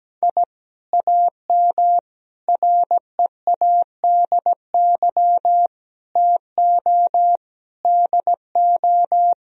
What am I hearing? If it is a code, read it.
Morse code, heard as a keyed tone, about 17 wpm, 700 Hz. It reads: I AM READY TO DO
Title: Sentences from Top 1000 Words - Review Part 2 - 17wpm